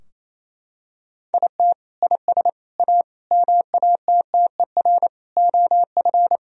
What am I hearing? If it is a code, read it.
Morse code, heard as a keyed tone, about 28 wpm, 700 Hz. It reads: IT IS A MATTER OF
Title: Sentences from Top 1000 Words - Review Part 3 - 28wpm